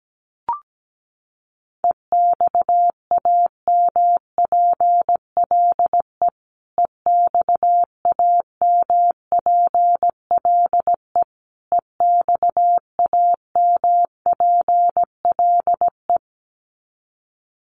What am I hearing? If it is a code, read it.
Morse code, heard as a keyed tone, about 17 wpm, 700 Hz. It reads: EXAMPLE EXAMPLE EXAMPLE
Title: Top 500 Words Repeated 17wpm